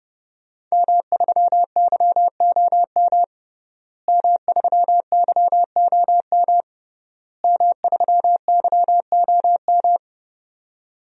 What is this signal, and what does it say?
Morse code, heard as a keyed tone, about 30 wpm, 700 Hz. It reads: M3YOM M3YOM M3YOM
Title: DXCC-100 Call Signs - 5 Characters - Repeated 3x - 30wpm